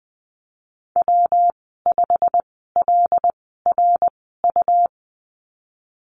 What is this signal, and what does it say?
Morse code, heard as a keyed tone, about 20 wpm, spaced as for 14 wpm, 700 Hz. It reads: W5LRU
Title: 154 Lesson 25g - Letter p - Call Signs 20wpm